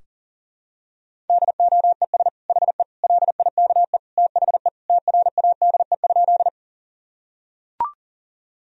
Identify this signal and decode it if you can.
Morse code, heard as a keyed tone, about 40 wpm, 700 Hz. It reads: DOES HE LIKE THE TRADE?
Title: Sentences from Top 1000 Words - Review Part 4 - 40wpm